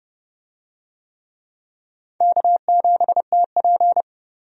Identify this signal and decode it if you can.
Morse code, heard as a keyed tone, about 30 wpm, 700 Hz. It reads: K7TP
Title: USA Call Signs - 4 Characters - 30wpm